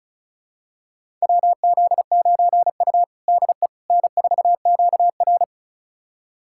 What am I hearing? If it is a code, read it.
Morse code, heard as a keyed tone, about 36 wpm, 700 Hz. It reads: WZ9U DE N4QR
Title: USA Callsign DE Callsign - 4 Characters - 35wpm